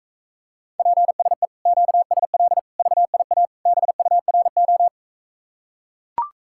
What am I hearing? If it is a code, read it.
Morse code, heard as a keyed tone, about 44 wpm, 700 Hz. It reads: PSE QSL VIA BURO